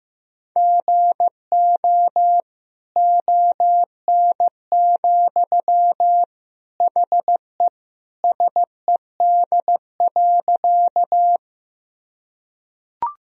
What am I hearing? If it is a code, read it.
Morse code, heard as a keyed tone, about 15 wpm, 700 Hz. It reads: GO ON, HE SED.
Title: Sentences from Top 100 Words and Abbreviations 15wpm